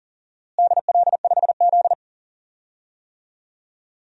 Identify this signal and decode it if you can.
Morse code, heard as a keyed tone, about 40 wpm, 700 Hz. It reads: DL5Z
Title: DXCC-100 Call Signs - 4 Characters - Repeated 2x - 40wpm